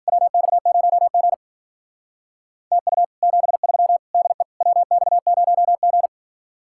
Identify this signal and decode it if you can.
Morse code, heard as a keyed tone, about 56 wpm, 700 Hz. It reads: WX0G TU 73 DE WX0G